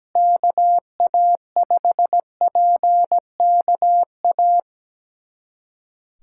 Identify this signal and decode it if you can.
Morse code, heard as a keyed tone, about 17 wpm, 700 Hz. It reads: KA5PKA
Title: USA Callsign DE Callsign - 6 Characters - 17wpm